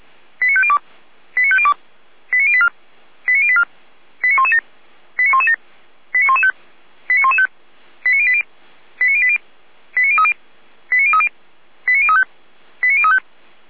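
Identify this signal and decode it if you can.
Five-tone selective call (ZVEI), heard as quick runs of five tones, tones of about 70 ms. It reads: 89502 89502 89084 89084 89117 89117 89115 89115 89080 89080 89030 89030 89035 89035
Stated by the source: ZVEI five-tone call 89094 received by SDR.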